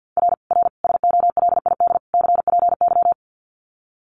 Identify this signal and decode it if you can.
Morse code, heard as a keyed tone, about 50 wpm, 700 Hz. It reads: R R SOLID CPY
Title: Common QSO Phrases 50wpm